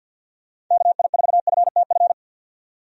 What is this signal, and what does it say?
Morse code, heard as a keyed tone, about 50 wpm, 700 Hz. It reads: KI4FTF
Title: USA Call Signs - 6 Characters - 50wpm